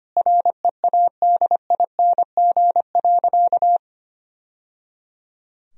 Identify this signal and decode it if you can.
Morse code, heard as a keyed tone, about 25 wpm, 700 Hz. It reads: READING.
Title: Sentences from Top 1000 Words - Review Part 3 - 25wpm